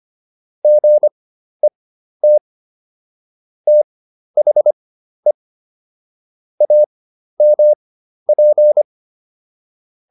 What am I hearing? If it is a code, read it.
Morse code, heard as a keyed tone, about 25 wpm, spaced as for 12 wpm, 600 Hz. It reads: GET THE AMP